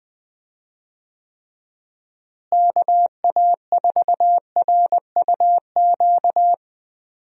Morse code, heard as a keyed tone, about 19 wpm, 700 Hz. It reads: KA4RUQ
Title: USA Call Signs - 6 Characters - Repeated 2x - 20wpm